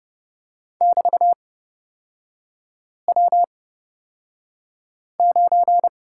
Morse code, heard as a keyed tone, about 30 wpm, spaced as for 5 wpm, 700 Hz. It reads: <BT>W9
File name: Lesson 36e - BT - Three Characters 30wpm